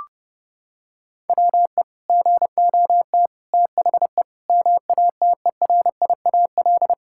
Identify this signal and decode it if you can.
Morse code, heard as a keyed tone, about 32 wpm, 700 Hz. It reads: WE GOT THE MATERIAL
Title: Sentences from Top 1000 Words - Review Part 5 - 30wpm